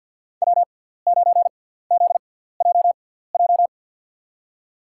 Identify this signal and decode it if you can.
Morse code, heard as a keyed tone, about 52 wpm, spaced as for 17 wpm, 700 Hz. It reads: W9ZJJ
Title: USA Call Signs - 5 Characters - 6x Character Spacing - 50wpm